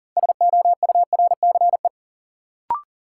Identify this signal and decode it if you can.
Morse code, heard as a keyed tone, about 37 wpm, 700 Hz. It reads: SOURCE
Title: Sentences from Top 800 Words - New Word Drill - 40wpm